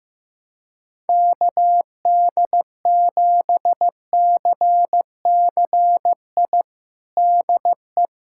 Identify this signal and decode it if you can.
Morse code, heard as a keyed tone, about 15 wpm, 700 Hz. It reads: KD7CCI DE
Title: USA Callsign DE Callsign - 6 Characters - 15wpm